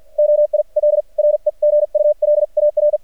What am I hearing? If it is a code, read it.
Morse code, heard as a keyed tone, about 46 wpm, spaced as for 40 wpm, 600 Hz. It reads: OIL REGULAR
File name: archived / CRSnet